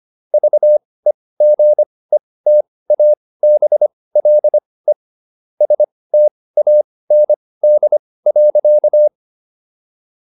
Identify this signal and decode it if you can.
Morse code, heard as a keyed tone, about 25 wpm, spaced as for 18 wpm, 600 Hz. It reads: VEGETABLE STAND.